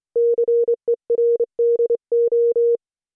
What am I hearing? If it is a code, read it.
Morse code, heard as a keyed tone, about 28 wpm, 470 Hz. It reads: CERDO